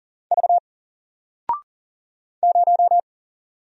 Morse code, heard as a keyed tone, about 40 wpm, 700 Hz. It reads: V 0
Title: Single Character (Letter:Number:Punctuation:Prosign) 40wpm